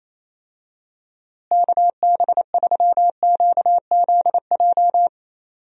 Morse code, heard as a keyed tone, about 28 wpm, 700 Hz. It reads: KB3QZJ
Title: USA Call Signs - 6 Characters - Repeated 3x - 28wpm